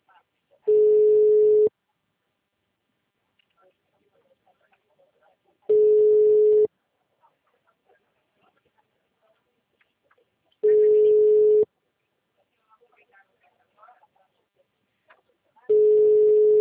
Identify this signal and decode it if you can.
Ringback tone (Europe, 425 Hz), with speech.